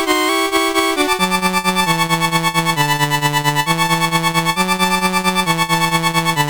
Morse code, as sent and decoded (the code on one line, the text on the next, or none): none